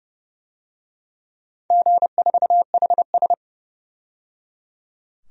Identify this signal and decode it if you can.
Morse code, heard as a keyed tone, about 28 wpm, 700 Hz. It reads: G4HS
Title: DXCC-100 Call Signs - 4 Characters - 30wpm